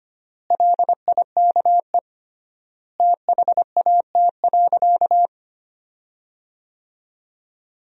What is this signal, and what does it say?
Morse code, heard as a keyed tone, about 25 wpm, 700 Hz. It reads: LIKE THAT.
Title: 3x Word-Spacing - Sentences from Top 300 Words - 25wpm